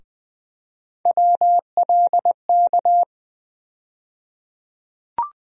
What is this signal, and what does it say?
Morse code, heard as a keyed tone, about 20 wpm, 700 Hz. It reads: WLK
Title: USA Call Sign Postfixes - 3 Letters 20wpm